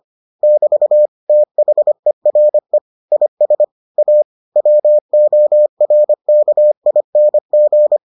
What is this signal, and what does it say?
Morse code, heard as a keyed tone, about 25 wpm, 600 Hz. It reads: <BT> THERE IS A WORKING